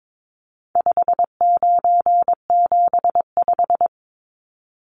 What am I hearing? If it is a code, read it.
Morse code, heard as a keyed tone, about 22 wpm, 700 Hz. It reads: 5975
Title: Numbers - 4-Digit 22wpm